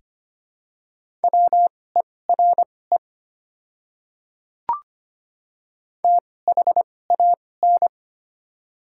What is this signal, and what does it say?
Morse code, heard as a keyed tone, about 25 wpm, spaced as for 18 wpm, 690 Hz. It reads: WERE THAN
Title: 2x Character Spacing - Top 300 Words - 25wpm